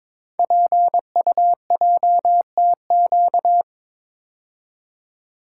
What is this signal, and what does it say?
Morse code, heard as a keyed tone, about 22 wpm, 700 Hz. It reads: PUJTQ
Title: Code Groups - 5 Letter:Numbers - 22wpm